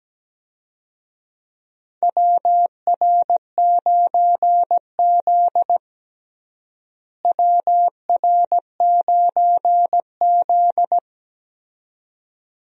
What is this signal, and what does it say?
Morse code, heard as a keyed tone, about 17 wpm, 700 Hz. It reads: WR9Z WR9Z
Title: USA Call Signs - 4 Characters - Repeated 2x - 17wpm